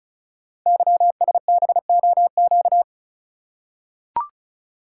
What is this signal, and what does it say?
Morse code, heard as a keyed tone, about 36 wpm, 700 Hz. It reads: YSBOQ
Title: Code Groups - 5 Letters - 35wpm